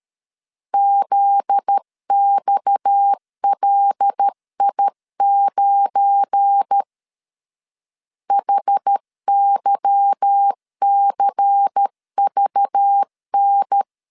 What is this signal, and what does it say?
Morse code, heard as a keyed tone, about 13 wpm, 800 Hz. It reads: ZXLI9 HYCVN
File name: LEZIONI / 9 28 12wpm 2351